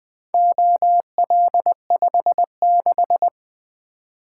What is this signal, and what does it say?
Morse code, heard as a keyed tone, about 20 wpm, 700 Hz. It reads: OL56
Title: Four Letter-Number Combinations - 20wpm